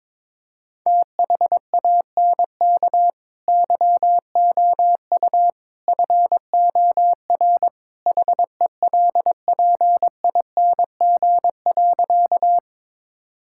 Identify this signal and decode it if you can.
Morse code, heard as a keyed tone, about 22 wpm, 700 Hz. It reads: THANK YOU FOR HELPING.
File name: practice / 22wpm / Sentences with 3-Word Phrases - 22wpm